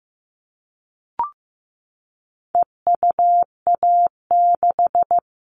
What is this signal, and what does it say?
Morse code, heard as a keyed tone, about 15 wpm, 700 Hz. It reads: EUA6
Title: All Cut Numbers - 4-Digit 15wpm